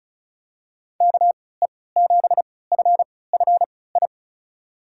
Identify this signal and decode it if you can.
Morse code, heard as a keyed tone, about 36 wpm, spaced as for 20 wpm, 700 Hz. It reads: KE7FFI
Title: USA Call Signs - 6 Characters - 3x Character Spacing - 35wpm